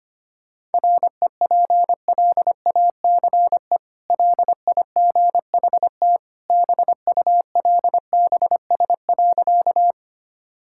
Morse code, heard as a keyed tone, about 25 wpm, 700 Hz. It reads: REPLACE LIGHT BULBS.
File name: Easy Sentences with Hints - 25wpm